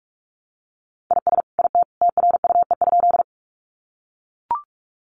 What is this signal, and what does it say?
Morse code, heard as a keyed tone, about 44 wpm, 700 Hz. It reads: IS IT TRUE?
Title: Sentences from Top 400 Words 45wpm